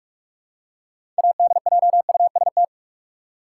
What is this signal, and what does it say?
Morse code, heard as a keyed tone, about 47 wpm, 700 Hz. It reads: ADJUST
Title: Six Letter Words - 45wpm